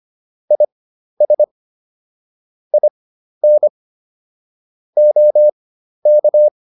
Morse code, heard as a keyed tone, about 25 wpm, spaced as for 12 wpm, 600 Hz. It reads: IS IN OK